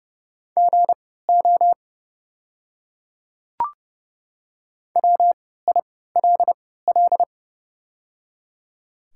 Morse code, heard as a keyed tone, about 31 wpm, spaced as for 17 wpm, 690 Hz. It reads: GO WILL
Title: 3x Character Spacing - Top 200 Words - 30wpm